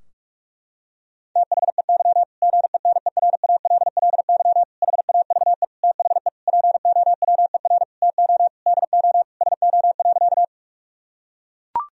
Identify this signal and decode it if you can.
Morse code, heard as a keyed tone, about 43 wpm, 700 Hz. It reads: THEY GENERALLY HAVE THE POWER TO DO SO.
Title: Sentences from Top 1000 Words - Review Part 4 - 45wpm